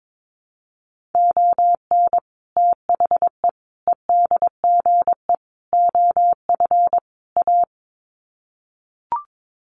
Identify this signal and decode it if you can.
Morse code, heard as a keyed tone, about 22 wpm, 700 Hz. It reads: ON THE EDGE OF A
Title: Sets of 5 Words Encore2 22wpm